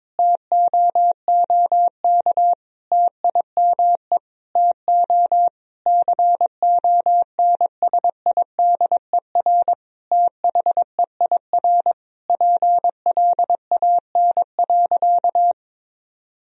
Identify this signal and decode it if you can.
Morse code, heard as a keyed tone, about 22 wpm, 700 Hz. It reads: TOOK TIME TO CONSIDER THEIR PLAN.
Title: Sentences from Top 1000 Words - Review Part 4 - 22wpm